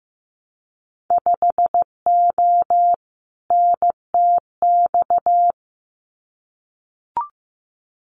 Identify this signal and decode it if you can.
Morse code, heard as a keyed tone, about 15 wpm, 700 Hz. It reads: 5O NTX
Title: Winter Field Day Exchanges Encore 15wpm